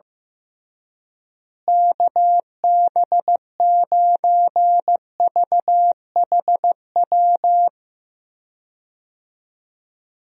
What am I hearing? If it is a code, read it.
Morse code, heard as a keyed tone, about 15 wpm, 700 Hz. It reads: KB9VHW
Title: USA Call Signs - 6 Characters - Repeated 2x - 15wpm